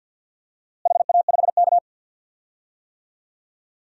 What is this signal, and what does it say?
Morse code, heard as a keyed tone, about 50 wpm, 700 Hz. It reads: HA5K
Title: DXCC-100 Call Signs - 4 Characters - Repeated 2x - 50wpm